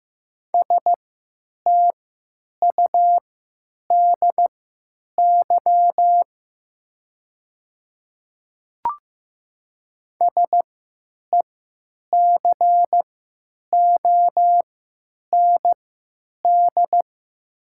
Morse code, heard as a keyed tone, about 15 wpm, spaced as for 9 wpm, 700 Hz. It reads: STUDY SECOND
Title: 3x Character Spacing - Top 500 Words - 15wpm